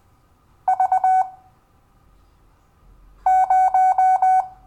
Morse code, heard as a keyed tone, about 20 wpm, spaced as for 4 wpm, 750 Hz. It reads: V0
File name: cw / Lesson12 YV